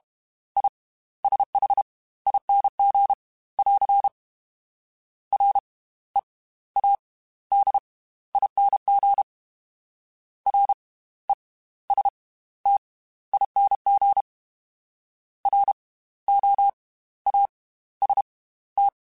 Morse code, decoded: ISHING <AR> READING RESTING ROAST

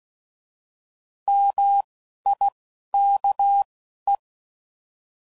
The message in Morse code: MIKE

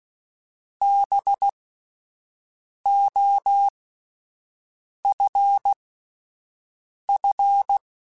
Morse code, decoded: BOFF

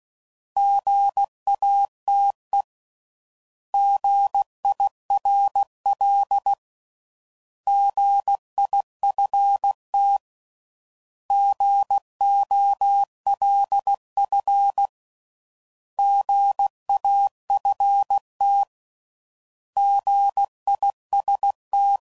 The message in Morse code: GATE GIRL GIFT GOLF GAFT GIST

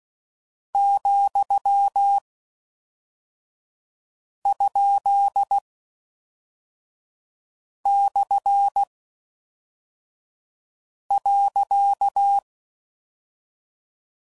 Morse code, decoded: ,?/.

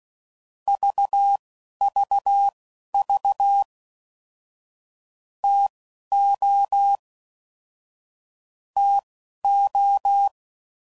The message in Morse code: VVV TO TO

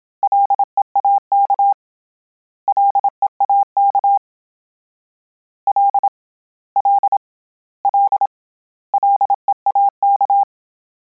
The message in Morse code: LEAK LEAK L L L LEAK

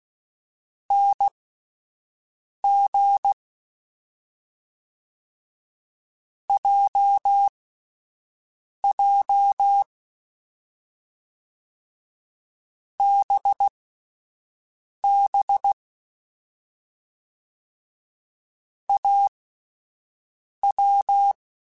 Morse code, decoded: NG JJ BB AW

